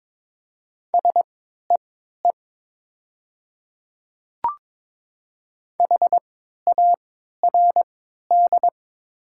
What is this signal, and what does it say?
Morse code, heard as a keyed tone, about 22 wpm, spaced as for 12 wpm, 700 Hz. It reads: SEE HARD
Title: 3x Character Spacing - Top 300 Words - 22wpm